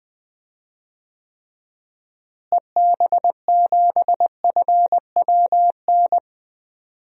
Morse code, heard as a keyed tone, about 20 wpm, 700 Hz. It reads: EB7FWN
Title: DXCC-100 Call Signs - 6 Characters - Repeated 2x - 20wpm